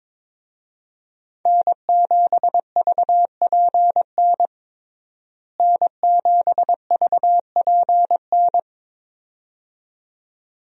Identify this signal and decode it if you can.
Morse code, heard as a keyed tone, about 23 wpm, 700 Hz. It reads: N7VPN N7VPN
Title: USA Call Signs - 5 Characters - Repeated 2x - 22wpm